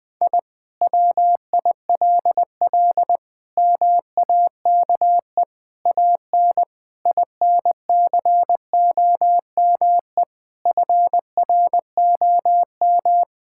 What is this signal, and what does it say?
Morse code, heard as a keyed tone, about 20 wpm, 700 Hz. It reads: I WILL MAKE AN INCOME FROM
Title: Sentences from Top 1000 Words - Review Part 5 - 20wpm